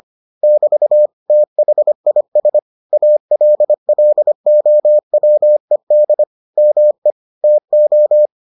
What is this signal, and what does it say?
Morse code, heard as a keyed tone, about 25 wpm, 600 Hz. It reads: <BT> THIS ALLOWED ME TO